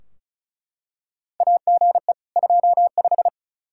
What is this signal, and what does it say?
Morse code, heard as a keyed tone, about 36 wpm, 700 Hz. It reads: AGE 25
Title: Practice QSO - Exchange 3 - More Info - 35wpm